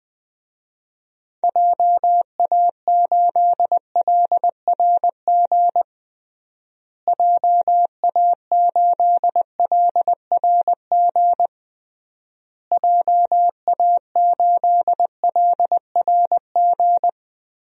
Morse code, heard as a keyed tone, about 21 wpm, 700 Hz. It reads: JA8LRG JA8LRG JA8LRG
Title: DXCC-100 Call Signs - 6 Characters - Repeated 3x - 20wpm